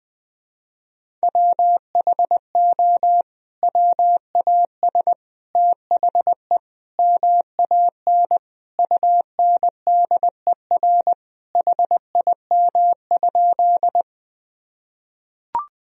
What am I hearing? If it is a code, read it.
Morse code, heard as a keyed tone, about 21 wpm, 700 Hz. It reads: WHO WAS THE MAN UNDER HIM?
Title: Sentences from Top 1000 Words - Review Part 1 - 20wpm